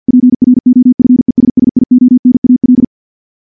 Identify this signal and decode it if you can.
Morse code, heard as a keyed tone, about 48 wpm, 250 Hz. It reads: PROFESSIONAL
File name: morse / morse-training-50WPM